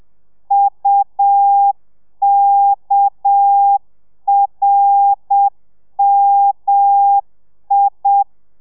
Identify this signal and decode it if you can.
Morse code, heard as a keyed tone, about 7 wpm, 800 Hz. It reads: UKRMI